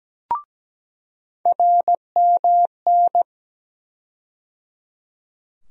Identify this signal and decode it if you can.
Morse code, heard as a keyed tone, about 17 wpm, 700 Hz. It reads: RMN
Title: USA Call Sign Postfixes - 3 Letters 17wpm